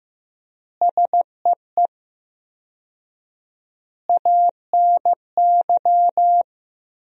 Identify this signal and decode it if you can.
Morse code, heard as a keyed tone, about 14 wpm, 700 Hz. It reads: SEE ANY
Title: 4x Word-Spacing - Sentences from Top 500 Words - 15wpm